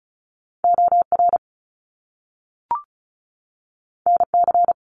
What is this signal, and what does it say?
Morse code, heard as a keyed tone, about 36 wpm, 700 Hz. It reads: OR NC
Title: Top Two Letter Combinations in Words 35wpm